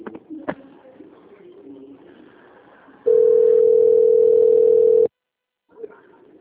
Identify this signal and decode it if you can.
Ringback tone (North America), 440 + 480 Hz.